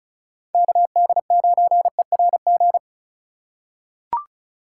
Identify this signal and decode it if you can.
Morse code, heard as a keyed tone, about 34 wpm, 700 Hz. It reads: KD9ERG
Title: USA Call Signs - 6 Characters - 3x Character Spacing - 35wpm